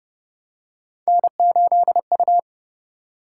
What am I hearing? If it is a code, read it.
Morse code, heard as a keyed tone, about 30 wpm, 700 Hz. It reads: N8U